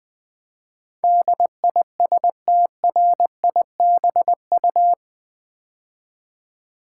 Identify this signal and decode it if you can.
Morse code, heard as a keyed tone, about 20 wpm, 700 Hz. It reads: DISTRIBU
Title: Build A Word - 4 Letters At A Time - 12-Letter Words - 20wpm